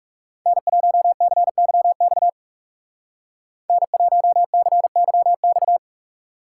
Morse code, heard as a keyed tone, about 35 wpm, 700 Hz. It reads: N1CYX N1CYX
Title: Speed Racing - USA 1x3 Call Signs 30wpm